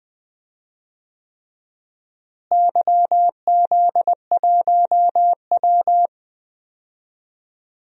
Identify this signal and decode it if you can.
Morse code, heard as a keyed tone, about 20 wpm, 700 Hz. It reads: YZ1W
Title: DXCC-100 Call Signs - 4 Characters - Repeated 3x - 20wpm